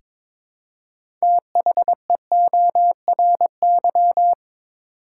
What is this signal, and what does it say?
Morse code, heard as a keyed tone, about 22 wpm, 700 Hz. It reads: THEORY
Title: Six Letter Words - 22wpm